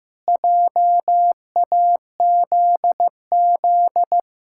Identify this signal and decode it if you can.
Morse code, heard as a keyed tone, about 15 wpm, 700 Hz. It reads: JAZZ
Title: Pangram Challenge v2 - 15wpm